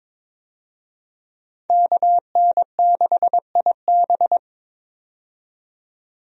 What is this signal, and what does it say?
Morse code, heard as a keyed tone, about 22 wpm, 700 Hz. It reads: KN6IB
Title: USA Call Signs - 5 Characters - 22wpm